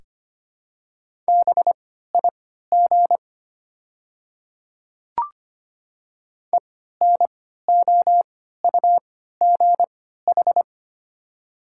Morse code, heard as a keyed tone, about 25 wpm, spaced as for 14 wpm, 700 Hz. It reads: BIG ENOUGH
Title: 3x Character Spacing - Top 500 Words - 25wpm